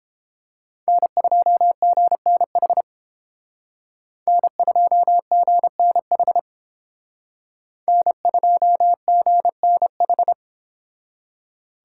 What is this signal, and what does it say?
Morse code, heard as a keyed tone, about 31 wpm, 700 Hz. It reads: N2GNH N2GNH N2GNH